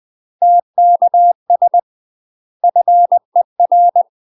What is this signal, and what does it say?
Morse code, heard as a keyed tone, about 20 wpm, 700 Hz. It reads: TKS FER
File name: UK MASTER QSOs TRAINING / 20WPM / UK MASTER QSOs 0067